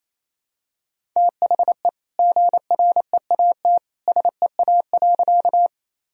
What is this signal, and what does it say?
Morse code, heard as a keyed tone, about 28 wpm, 700 Hz. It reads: THE GREAT SEA.
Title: Sentences from Top 1000 Words - Review Part 1 - 28wpm